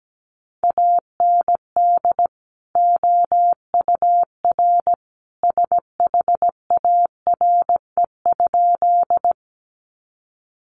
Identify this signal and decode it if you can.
Morse code, heard as a keyed tone, about 17 wpm, 700 Hz. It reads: AND OUR SHARE?
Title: Sentences from Top 900 Words 17wpm